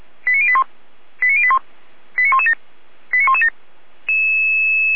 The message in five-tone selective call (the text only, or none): 89081 89081 89117 89117